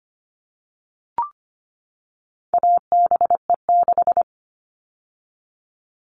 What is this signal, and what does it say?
Morse code, heard as a keyed tone, about 25 wpm, 700 Hz. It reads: ABE6
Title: All Cut Numbers - 4-Digit 25wpm